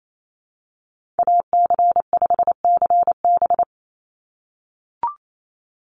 Morse code, heard as a keyed tone, about 28 wpm, 700 Hz. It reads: AC5CB
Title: USA Call Signs 28wpm